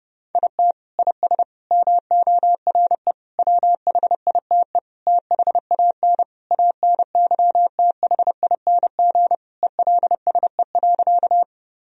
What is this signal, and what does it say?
Morse code, heard as a keyed tone, about 30 wpm, 700 Hz. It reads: IT IS MORE WHITE THAN ANYTHING ELSE.
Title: Sentences from Top 1000 Words - Review Part 1 - 30wpm